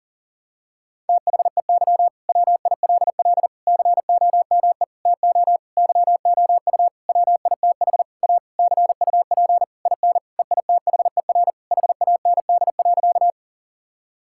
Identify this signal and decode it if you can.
Morse code, heard as a keyed tone, about 40 wpm, 700 Hz. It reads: THEY WILL COME TO YOU WITH A CUP IN EITHER HAND.
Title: Sentences from Top 1600 Words 40wpm